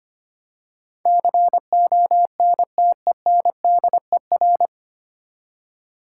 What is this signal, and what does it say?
Morse code, heard as a keyed tone, about 25 wpm, 700 Hz. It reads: CONTENDER
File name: Nine Letter Words - 25wpm